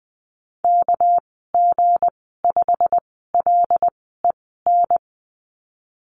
Morse code, heard as a keyed tone, about 20 wpm, spaced as for 15 wpm, 700 Hz. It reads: KG5LEN
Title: 231 Lesson 37g - Letter x - Call Signs 20wpm